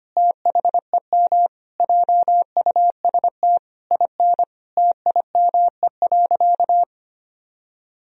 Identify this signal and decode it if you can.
Morse code, heard as a keyed tone, about 25 wpm, 700 Hz. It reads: THEM JUST IN TIME.
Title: Sentences from Top 1000 Words - Review Part 2 - 25wpm